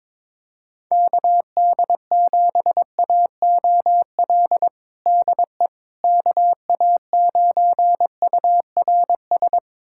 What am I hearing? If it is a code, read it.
Morse code, heard as a keyed tone, about 22 wpm, 700 Hz. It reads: KD7AOL DE KA9URS
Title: USA Callsign DE Callsign 22wpm